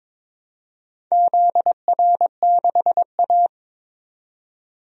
Morse code, heard as a keyed tone, about 22 wpm, 700 Hz. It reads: ZR6A
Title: DXCC-100 Call Signs - 4 Characters - 22wpm